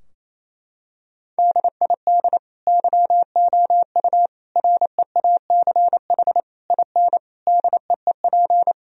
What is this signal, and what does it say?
Morse code, heard as a keyed tone, about 28 wpm, 700 Hz. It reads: DID YOU REACH IN DEEP
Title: Sentences from Top 1000 Words - Review Part 3 - 28wpm